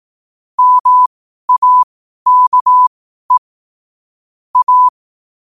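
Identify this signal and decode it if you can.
Morse code, heard as a keyed tone, about 18 wpm, spaced as for 12 wpm, 1000 Hz. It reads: MAKE A
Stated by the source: 12 wpm Morse.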